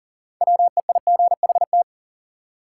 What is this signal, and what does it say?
Morse code, heard as a keyed tone, about 37 wpm, 700 Hz. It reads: WEIGHT
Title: Six Letter Words - 40wpm